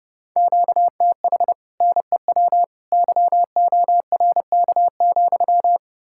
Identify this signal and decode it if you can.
Morse code, heard as a keyed tone, about 37 wpm, 700 Hz. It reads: QTH NEW YORK,
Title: Speed-Racing Common QSO Phrases - 30wpm